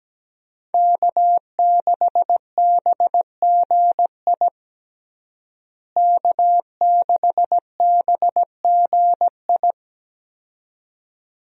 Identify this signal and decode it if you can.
Morse code, heard as a keyed tone, about 17 wpm, 700 Hz. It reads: K6BGI K6BGI
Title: USA Call Signs - 5 Characters - Repeated 2x - 17wpm